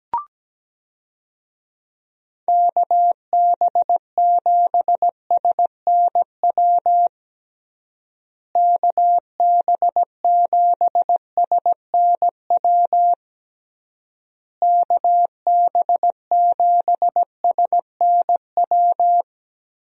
Morse code, heard as a keyed tone, about 17 wpm, 700 Hz. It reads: KB7SNW KB7SNW KB7SNW
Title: USA Call Signs - 6 Characters - Repeated 3x - 17wpm